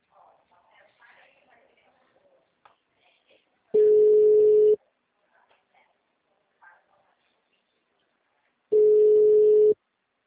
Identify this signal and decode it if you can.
Ringback tone (Europe, 425 Hz), with speech.